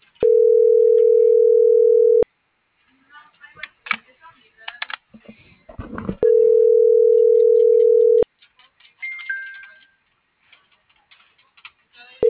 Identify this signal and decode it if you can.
Ringback tone (North America), 440 + 480 Hz.
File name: CENTRAL MABE / out david.toribio